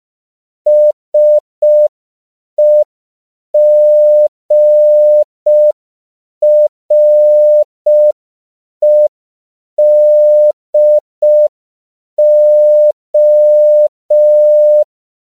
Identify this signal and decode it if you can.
Morse code, heard as a keyed tone, about 5 wpm, 600 Hz. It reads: SEGREDO
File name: 5PPM